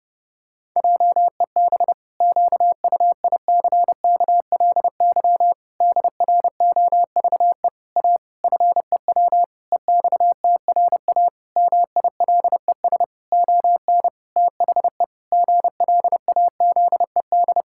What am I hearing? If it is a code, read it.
Morse code, heard as a keyed tone, about 30 wpm, 700 Hz. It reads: JEB QUICKLY DROVE A FEW EXTRA MILES ON THE GLAZED
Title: Pangram Challenge v2 - 30wpm